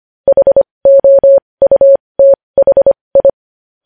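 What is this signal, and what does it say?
Morse code, heard as a keyed tone, about 25 wpm, spaced as for 20 wpm, 550 Hz. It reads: HOUTHI
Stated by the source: Morse code practice of news headlines at 20 wpm.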